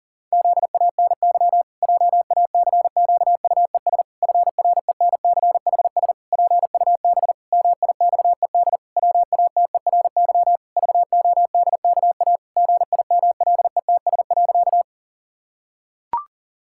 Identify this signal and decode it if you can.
Morse code, heard as a keyed tone, about 40 wpm, 700 Hz. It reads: ZANY JACQUES FRENCHS PUB MIXED WATERY VODKA GIMLETS.
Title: Pangram Challenge v2 - 40wpm